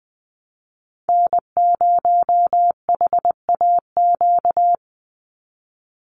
Morse code, heard as a keyed tone, about 20 wpm, 700 Hz. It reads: N0HAQ
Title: USA Call Signs 20wpm